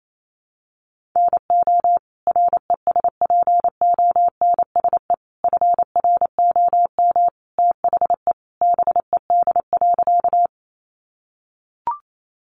Morse code, heard as a keyed tone, about 28 wpm, 700 Hz. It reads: NO RESPONSE FROM THE BED.
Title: Sentences from Top 1400 Words 28wpm